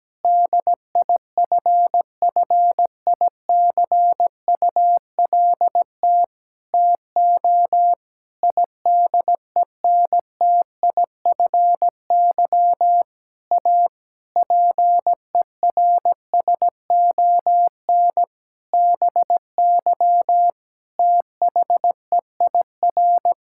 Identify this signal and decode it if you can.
Morse code, heard as a keyed tone, about 17 wpm, 700 Hz. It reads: DIFFICULT TO IDENTIFY A PERSON BY THEIR
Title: Sentences from Top 1000 Words - Review Part 6 - 17wpm